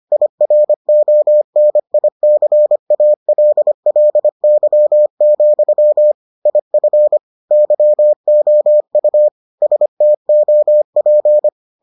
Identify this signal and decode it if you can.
Morse code, heard as a keyed tone, about 25 wpm, 600 Hz. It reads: IRONICALLY, IF YOU STOP